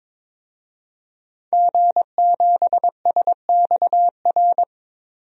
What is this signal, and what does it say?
Morse code, heard as a keyed tone, about 21 wpm, 700 Hz. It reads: G7SXR
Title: DXCC-100 Call Signs - 5 Characters - 22wpm